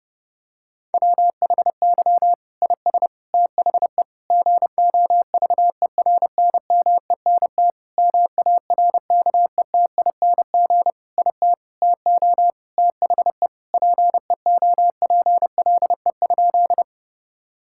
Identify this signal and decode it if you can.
Morse code, heard as a keyed tone, about 30 wpm, 700 Hz. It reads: WHY IS THE GOVERNMENT MARKETING IT TO THE PEOPLE?
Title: Sentences from Top 1000 Words - Review Part 5 - 30wpm